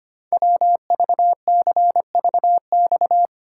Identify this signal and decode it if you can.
Morse code, heard as a keyed tone, about 25 wpm, 700 Hz. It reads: WVCVX
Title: Code Groups - 5 Letter:Numbers - 25wpm